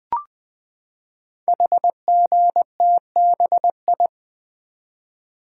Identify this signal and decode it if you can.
Morse code, heard as a keyed tone, about 20 wpm, 700 Hz. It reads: HGTBI
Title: Code Groups - 5 Letters - 20wpm